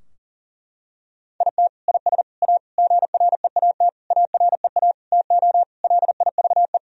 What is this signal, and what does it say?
Morse code, heard as a keyed tone, about 40 wpm, 700 Hz. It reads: IT IS A GREAT AREA TO LIVE
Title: Sentences from Top 1000 Words - Review Part 2 - 40wpm